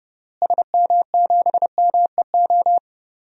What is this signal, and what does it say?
Morse code, heard as a keyed tone, about 30 wpm, 700 Hz. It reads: SM7MEO
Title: DXCC-100 Call Signs - 6 Characters - Repeated 3x - 30wpm